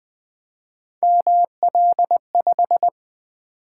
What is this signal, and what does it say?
Morse code, heard as a keyed tone, about 20 wpm, 700 Hz. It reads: ML5
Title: Three Letter-Number Combinations - 20wpm